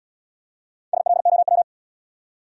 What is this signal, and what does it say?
Morse code, heard as a keyed tone, about 75 wpm, 700 Hz. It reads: SLOW